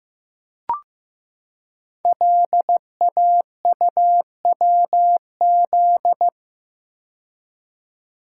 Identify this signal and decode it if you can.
Morse code, heard as a keyed tone, about 15 wpm, 700 Hz. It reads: LAUWZ
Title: Code Groups - 5 Letter:Numbers - 15wpm